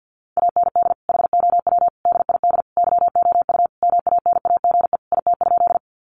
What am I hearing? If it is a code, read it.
Morse code, heard as a keyed tone, about 50 wpm, 700 Hz. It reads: AND HOW DID YOU MANAGE IT?
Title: Sentences from Top 1400 Words 50wpm